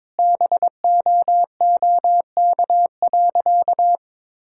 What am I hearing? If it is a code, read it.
Morse code, heard as a keyed tone, about 22 wpm, 700 Hz. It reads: BOOK.
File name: practice / 22wpm / Sentences with 2-Word Phrases - 22wpm